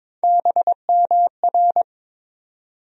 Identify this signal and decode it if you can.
Morse code, heard as a keyed tone, about 22 wpm, 700 Hz. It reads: BMR
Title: USA Call Sign Postfixes - 3 Letters 22wpm